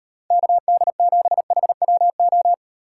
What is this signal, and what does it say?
Morse code, heard as a keyed tone, about 38 wpm, 700 Hz. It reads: KD7HWO